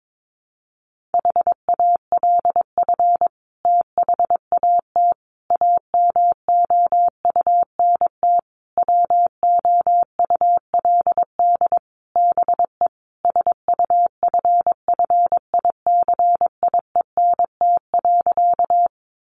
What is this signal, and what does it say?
Morse code, heard as a keyed tone, about 22 wpm, 700 Hz. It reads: HALF THAT AMOUNT WOULD BE SUFFICIENT.
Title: Sentences from Top 1400 Words 22wpm